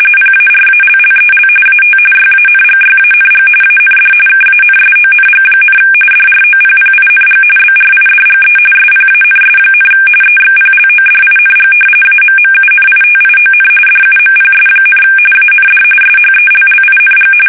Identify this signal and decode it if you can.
RTTY, about 45.45 baud. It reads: PARALLEL AT YANGYANG AND KANSONG, DROPPING BRIDGE SPANS, EXPLODING AMMUNITION DUMPS, AND SHELLING
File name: fsk850 / battleshipnj